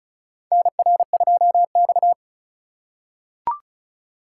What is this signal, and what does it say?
Morse code, heard as a keyed tone, about 36 wpm, 700 Hz. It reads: NR2X
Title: USA Call Signs - 4 Characters - 4x Character Spacing - 35wpm